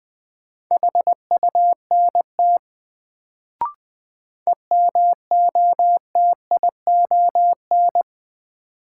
Morse code, heard as a keyed tone, about 20 wpm, 700 Hz. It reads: HUNT EMOTION
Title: Sentences from Top 1600 Words - New Word Drill Alt - 20wpm